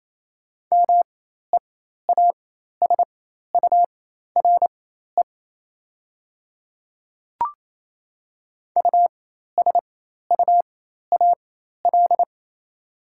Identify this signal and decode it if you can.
Morse code, heard as a keyed tone, about 28 wpm, spaced as for 13 wpm, 700 Hz. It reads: MEASURE USUAL
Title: 4x Character Spacing - Top 500 Words - 28wpm